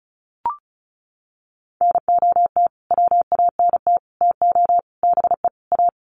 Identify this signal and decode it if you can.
Morse code, heard as a keyed tone, about 34 wpm, 700 Hz. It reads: NOT WANT TO BE A
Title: Sets of 5 Words Encore 35wpm